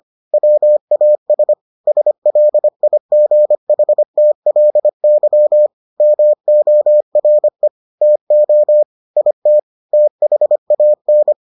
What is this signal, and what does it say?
Morse code, heard as a keyed tone, about 25 wpm, 600 Hz. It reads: WAS SLIGHTLY MORE TO IT THAN